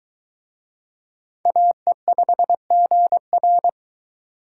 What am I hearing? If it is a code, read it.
Morse code, heard as a keyed tone, about 20 wpm, 700 Hz. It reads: AE5GR